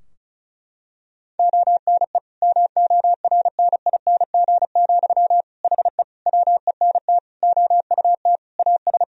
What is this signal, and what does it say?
Morse code, heard as a keyed tone, about 36 wpm, 700 Hz. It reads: ONE MORNING, HE WENT OUT AS
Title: Sentences from Top 1000 Words - Review Part 2 - 35wpm